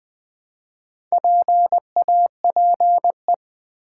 Morse code, heard as a keyed tone, about 20 wpm, 700 Hz. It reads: PAPE
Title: Build A Word - 1 Letter At A Time - 3 to 5-Letter Words - 20wpm